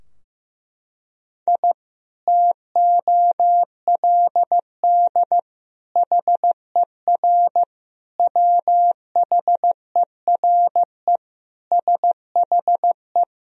Morse code, heard as a keyed tone, about 15 wpm, 700 Hz. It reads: I TOLD HER WHERE SHE
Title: Sentences from Top 1000 Words - Review Part 2 - 15wpm